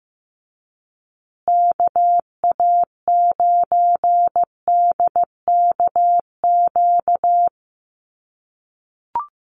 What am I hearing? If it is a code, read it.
Morse code, heard as a keyed tone, about 15 wpm, 700 Hz. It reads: KA9DKQ
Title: USA Call Signs Repeated 3x 15wpm